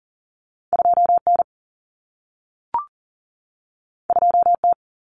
Morse code, heard as a keyed tone, about 40 wpm, 700 Hz. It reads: 2N 2T
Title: Common Cut Numbers - 2-Digit 40wpm